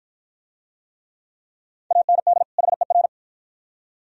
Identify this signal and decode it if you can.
Morse code, heard as a keyed tone, about 52 wpm, 700 Hz. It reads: AND HER